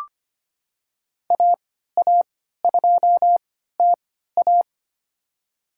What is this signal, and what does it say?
Morse code, heard as a keyed tone, about 25 wpm, spaced as for 14 wpm, 700 Hz. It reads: AA2TA
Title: USA Call Signs - 5 Characters - 3x Character Spacing - 25wpm